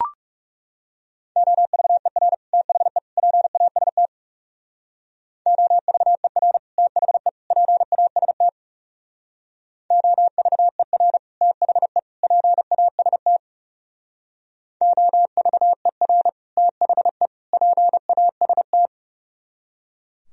Morse code, heard as a keyed tone, about 40 wpm, 700 Hz. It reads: OVER THE PAST OVER THE PAST OVER THE PAST OVER THE PAST